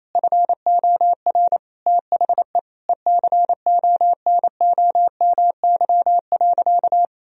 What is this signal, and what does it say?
Morse code, heard as a keyed tone, about 27 wpm, 700 Hz. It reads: FOR THE ECONOMY.
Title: Sentences from Top 1000 Words - Review Part 3 - 28wpm